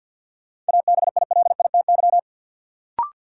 Morse code, heard as a keyed tone, about 52 wpm, 700 Hz. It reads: ABILITY
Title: Sentences from Top 1200 Words - New Word Drill Alt - 50wpm